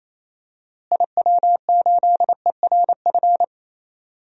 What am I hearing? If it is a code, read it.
Morse code, heard as a keyed tone, about 28 wpm, 700 Hz. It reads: IW8ERF
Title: DXCC-100 Call Signs - 6 Characters - Repeated 3x - 28wpm